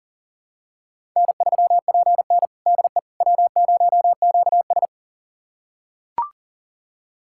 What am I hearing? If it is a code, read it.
Morse code, heard as a keyed tone, about 40 wpm, 700 Hz. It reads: N3PN DE W0QS
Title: USA Callsign DE Callsign - 4 Characters - 40wpm